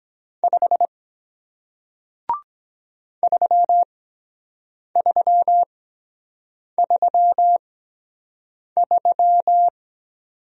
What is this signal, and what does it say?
Morse code, heard as a keyed tone, about 23 wpm, spaced as for 7 wpm, 700 Hz. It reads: 5 3333